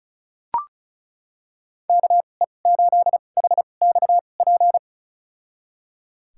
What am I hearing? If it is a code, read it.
Morse code, heard as a keyed tone, about 34 wpm, spaced as for 25 wpm, 700 Hz. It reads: KE8HXP
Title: USA Call Signs - 2x Character Spacing - 35wpm